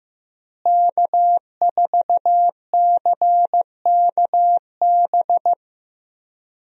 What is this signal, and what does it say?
Morse code, heard as a keyed tone, about 15 wpm, 700 Hz. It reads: K4CKB
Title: USA Call Signs - 5 Characters - Repeated 3x - 15wpm